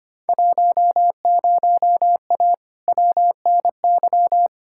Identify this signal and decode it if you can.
Morse code, heard as a keyed tone, about 25 wpm, 700 Hz. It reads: 10A WNY
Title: Field Day Exchanges - 25wpm